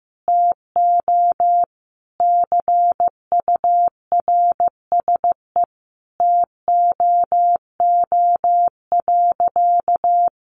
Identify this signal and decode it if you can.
Morse code, heard as a keyed tone, about 15 wpm, 700 Hz. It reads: TO CURSE TOO.
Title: A Sound Of Thunder 15wpm